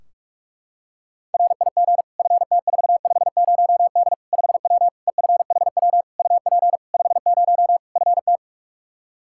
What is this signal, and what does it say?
Morse code, heard as a keyed tone, about 41 wpm, 700 Hz. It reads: RIG FT450D 5W EFHW UP 50 FT
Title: Practice QSO - Exchange 2 - Info - 45wpm